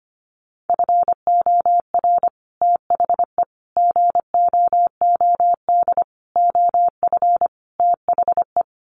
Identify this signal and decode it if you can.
Morse code, heard as a keyed tone, about 25 wpm, 700 Hz. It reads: FOR THE GOOD OF THE
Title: Sets of 5 Words Encore2 25wpm